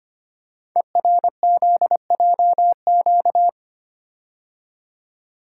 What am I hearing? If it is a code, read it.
Morse code, heard as a keyed tone, about 25 wpm, 700 Hz. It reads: ERZJQ
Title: Code Groups - 5 Letters - 25wpm